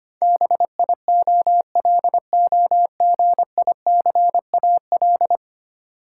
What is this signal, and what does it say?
Morse code, heard as a keyed tone, about 25 wpm, 700 Hz. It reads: BIOLOGICAL
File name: 10 Letter Words - First 3 Letters Given - 25wpm